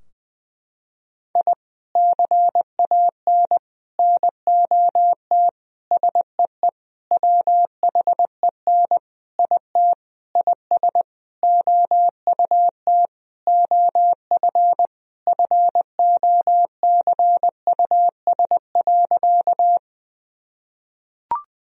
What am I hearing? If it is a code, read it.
Morse code, heard as a keyed tone, about 21 wpm, 700 Hz. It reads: I CAN NOT SEE WHEN IT IS OUT OF FOCUS.
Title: Sentences from Top 1000 Words - Review Part 4 - 20wpm